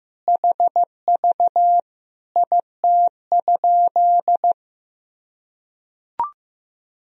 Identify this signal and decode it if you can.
Morse code, heard as a keyed tone, about 15 wpm, 700 Hz. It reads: HV IT?
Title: Sentences from Top 100 Words and Abbreviations 15wpm